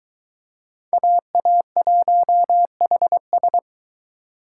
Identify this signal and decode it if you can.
Morse code, heard as a keyed tone, about 22 wpm, 700 Hz. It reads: AA1HS